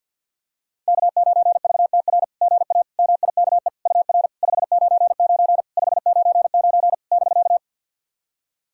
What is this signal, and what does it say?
Morse code, heard as a keyed tone, about 49 wpm, 690 Hz. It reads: K9VTR GA MIKE UR 599 599 <BK>
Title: SOTA QSO Practice 50wpm